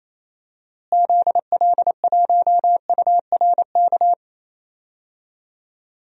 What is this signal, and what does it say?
Morse code, heard as a keyed tone, about 28 wpm, 700 Hz. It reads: ZL1URK
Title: DXCC-100 Call Signs - 6 Characters - Repeated 3x - 28wpm